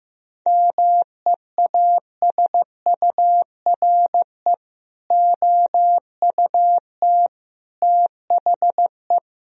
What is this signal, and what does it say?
Morse code, heard as a keyed tone, about 15 wpm, 700 Hz. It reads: MEASURE OUT THE
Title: Sentences from Top 1000 Words - Review Part 2 - 15wpm